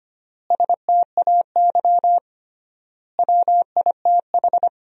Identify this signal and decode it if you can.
Morse code, heard as a keyed tone, about 25 wpm, 700 Hz. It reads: STAY WITH
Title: 3x Word-Spacing - Sentences from Top 500 Words - 25wpm